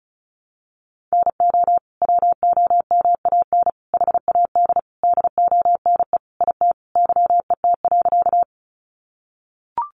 Morse code, heard as a keyed tone, about 34 wpm, 700 Hz. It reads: NO WOMAN HAD DONE IT YET.
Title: Sentences from Top 1100 Words 35wpm